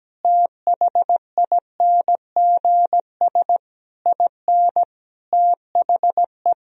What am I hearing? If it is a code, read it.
Morse code, heard as a keyed tone, about 17 wpm, 700 Hz. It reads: THINGS IN THE
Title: Sentences from Top 1000 Words - Review Part 4 - 17wpm